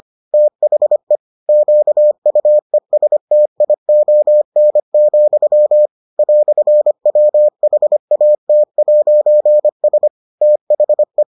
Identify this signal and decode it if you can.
Morse code, heard as a keyed tone, about 25 wpm, 600 Hz. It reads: THE QUESTION, "WHAT'S THE